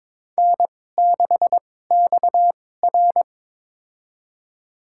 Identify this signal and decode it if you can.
Morse code, heard as a keyed tone, about 22 wpm, spaced as for 16 wpm, 700 Hz. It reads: N6XR
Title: USA Call Signs - 4 Characters - 2x Character Spacing - 22wpm